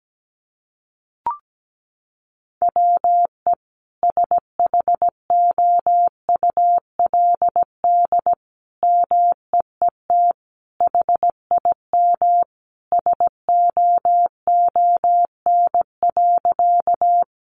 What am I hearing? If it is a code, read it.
Morse code, heard as a keyed tone, about 17 wpm, 700 Hz. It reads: WE SHOULD MEET HIM SOON.
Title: Sentences from Top 1100 Words 17wpm